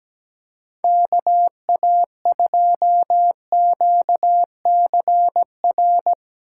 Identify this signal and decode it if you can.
Morse code, heard as a keyed tone, about 19 wpm, 700 Hz. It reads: KA2QCR
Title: Speed Racing - USA 2x3 Call Signs 17wpm